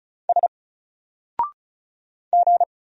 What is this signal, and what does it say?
Morse code, heard as a keyed tone, about 36 wpm, 700 Hz. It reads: S G
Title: Single Character (Letter:Number:Punctuation:Prosign) Alt 35wpm